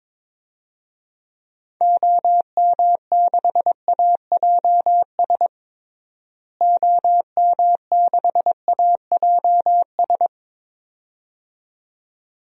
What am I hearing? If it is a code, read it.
Morse code, heard as a keyed tone, about 22 wpm, 700 Hz. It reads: OM6AJS OM6AJS
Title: DXCC-100 Call Signs - 6 Characters - Repeated 2x - 22wpm